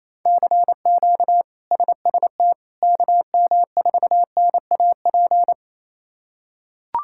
Morse code, heard as a keyed tone, about 27 wpm, 700 Hz. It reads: CQ SST KM4NAP
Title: K1USN SST QSO Practice - 28wpm